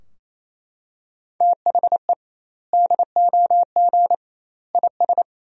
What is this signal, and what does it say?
Morse code, heard as a keyed tone, about 28 wpm, 700 Hz. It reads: THE DOG IS
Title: 2x Word-Spacing - Sentences from Top 400 Words - 28wpm